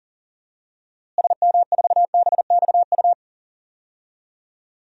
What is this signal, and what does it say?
Morse code, heard as a keyed tone, about 40 wpm, 700 Hz. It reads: SM4BXU
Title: DXCC-100 Call Signs - 6 Characters - Repeated 3x - 40wpm